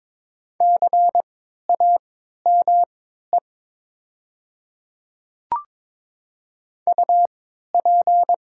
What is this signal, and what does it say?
Morse code, heard as a keyed tone, about 22 wpm, spaced as for 12 wpm, 700 Hz. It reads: CAME UP
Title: 3x Character Spacing - Top 500 Words - 22wpm